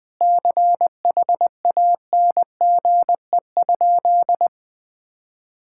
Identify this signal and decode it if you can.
Morse code, heard as a keyed tone, about 21 wpm, 700 Hz. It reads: CHANGE?
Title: Sentences from Top 1600 Words 20wpm